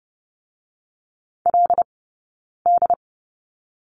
Morse code, heard as a keyed tone, about 30 wpm, 700 Hz. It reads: L D
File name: Lesson 19d - Letter w - Two Characters 30wpm